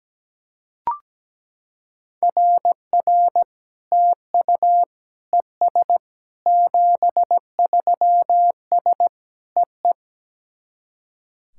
Morse code, heard as a keyed tone, about 17 wpm, 700 Hz. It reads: RR TU ES 73S EE
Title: SOTA QSO Practice 17wpm